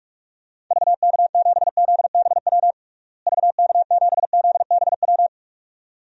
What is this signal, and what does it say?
Morse code, heard as a keyed tone, about 46 wpm, 700 Hz. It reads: VK7ZBW VK7ZBW
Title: DXCC-100 Call Signs - 6 Characters - Repeated 2x - 45wpm